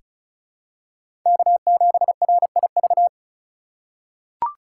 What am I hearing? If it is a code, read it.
Morse code, heard as a keyed tone, about 36 wpm, 700 Hz. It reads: K7RIV
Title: USA Call Signs - 4x Character Spacing - 35wpm